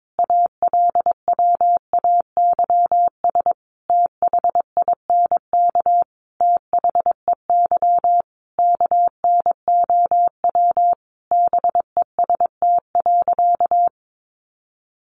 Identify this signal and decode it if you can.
Morse code, heard as a keyed tone, about 22 wpm, 700 Hz. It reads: ALWAYS THINK THEY KNOW BEST.
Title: Sentences from Top 600 Words 22wpm